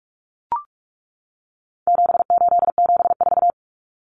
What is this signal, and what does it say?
Morse code, heard as a keyed tone, about 44 wpm, 700 Hz. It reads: 7874
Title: Common Cut Numbers - 4-Digit 45wpm